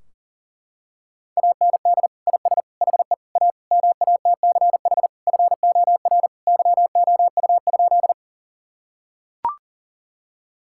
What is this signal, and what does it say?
Morse code, heard as a keyed tone, about 37 wpm, 700 Hz. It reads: AND IS HE A MATCH FOR YOU?
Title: Sentences from Top 1600 Words 40wpm